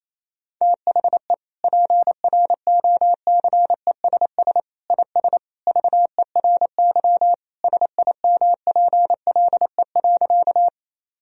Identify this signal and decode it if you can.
Morse code, heard as a keyed tone, about 28 wpm, 700 Hz. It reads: THE PROCESS IS VERY SIMPLE.
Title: Sentences from Top 1000 Words - Review Part 3 - 28wpm